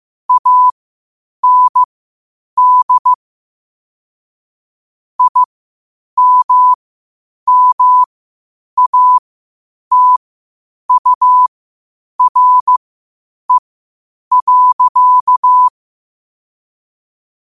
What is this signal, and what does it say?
Morse code, heard as a keyed tone, about 15 wpm, spaced as for 8 wpm, 1000 Hz. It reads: AND IMMATURE.
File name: qotdfiles / 8wpm